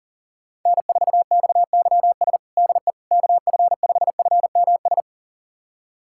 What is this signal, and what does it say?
Morse code, heard as a keyed tone, about 37 wpm, 700 Hz. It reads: N4XYS DE KF5FMS
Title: USA Callsign DE Callsign 40wpm